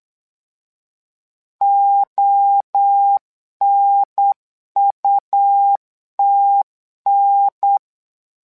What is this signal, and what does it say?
Morse code, heard as a keyed tone, about 8 wpm, 800 Hz. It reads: ONUTN
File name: LEZIONI / 3 MWRO / 3 MWRO 8wpm 2606